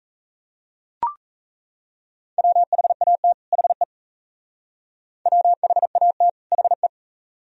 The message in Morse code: WHAT HE WHAT HE

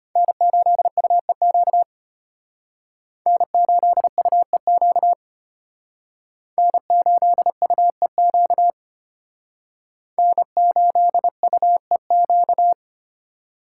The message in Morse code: N8UEQ N8UEQ N8UEQ N8UEQ

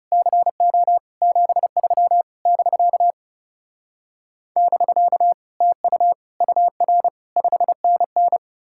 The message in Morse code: CO 73 <BK> <BK> TU UR 5NN

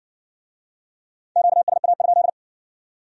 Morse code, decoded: QSA?

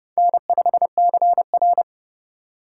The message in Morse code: N5CR

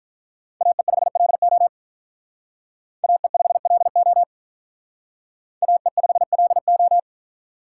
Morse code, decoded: AE5LO AE5LO AE5LO

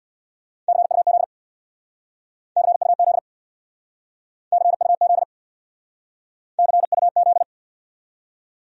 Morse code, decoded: CUZ CUZ CUZ CUZ

